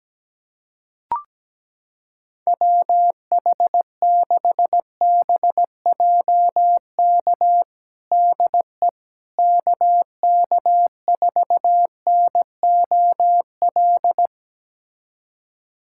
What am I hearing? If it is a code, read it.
Morse code, heard as a keyed tone, about 17 wpm, 700 Hz. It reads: WH6BJK DE KK4NOL
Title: USA Callsign DE Callsign 17wpm